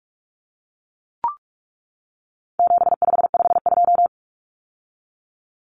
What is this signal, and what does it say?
Morse code, heard as a keyed tone, about 44 wpm, 700 Hz. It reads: 7552
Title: Common Cut Numbers - 4-Digit 45wpm